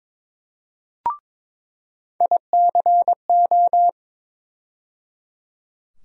Morse code, heard as a keyed tone, about 22 wpm, 700 Hz. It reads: ICO
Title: USA Call Sign Postfixes - 3 Letters 22wpm